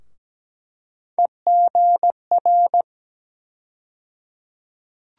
Morse code, heard as a keyed tone, about 17 wpm, 700 Hz. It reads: EGR
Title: USA Call Sign Postfixes - 3 Letters 17wpm